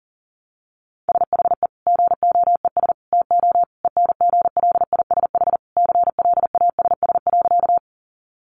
Morse code, heard as a keyed tone, about 40 wpm, 700 Hz. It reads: SHE GOES TO ENGLISH CLASS.
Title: Additional Easy Phrases 40wpm